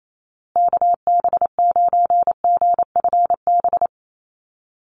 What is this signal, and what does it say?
Morse code, heard as a keyed tone, about 28 wpm, 700 Hz. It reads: KB9GFB
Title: USA Call Signs Repeated 2x 28wpm